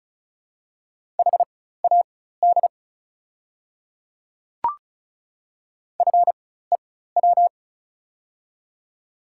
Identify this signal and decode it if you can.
Morse code, heard as a keyed tone, about 34 wpm, spaced as for 16 wpm, 700 Hz. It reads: HAD FEW
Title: 4x Character Spacing - Top 300 Words - 35wpm